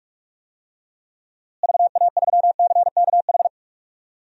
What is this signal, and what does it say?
Morse code, heard as a keyed tone, about 46 wpm, 700 Hz. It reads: VA3CKH
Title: DXCC-100 Call Signs - 6 Characters - Repeated 3x - 45wpm